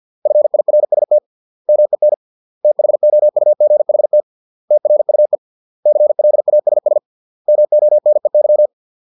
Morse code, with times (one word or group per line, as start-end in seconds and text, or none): 0.3-1.2 FIRST
1.7-2.1 MEN
2.6-4.2 THOUGHT
4.7-5.4 TRUE
5.9-7.0 CLASS
7.5-8.7 MONEY